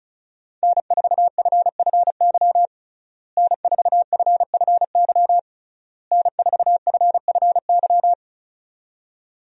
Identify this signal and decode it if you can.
Morse code, heard as a keyed tone, about 36 wpm, 700 Hz. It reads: N4FFY N4FFY N4FFY
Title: USA Call Signs - 5 Characters - Repeated 3x - 35wpm